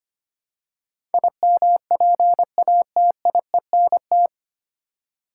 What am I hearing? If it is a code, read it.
Morse code, heard as a keyed tone, about 25 wpm, 700 Hz. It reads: IMPATIENT
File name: practice / 25wpm / 9 Letter Words - First 3 Letters Given - 25wpm